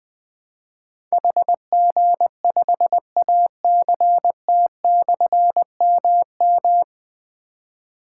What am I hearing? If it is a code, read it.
Morse code, heard as a keyed tone, about 20 wpm, 700 Hz. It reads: HG5ACT/MM
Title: Difficult Call Signs 20wpm